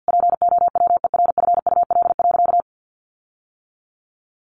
Morse code, heard as a keyed tone, about 50 wpm, 700 Hz. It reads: POWERFUL.
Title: Sentences from Top 1300 Words 50wpm